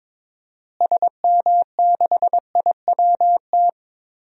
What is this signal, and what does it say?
Morse code, heard as a keyed tone, about 22 wpm, 700 Hz. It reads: SM6IWT
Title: DXCC-100 Call Signs - 6 Characters - Repeated 2x - 22wpm